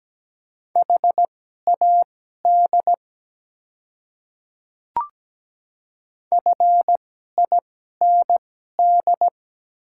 Morse code, heard as a keyed tone, about 17 wpm, spaced as for 12 wpm, 700 Hz. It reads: HAD FIND
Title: 2x Character Spacing - Top 100 Words - 17wpm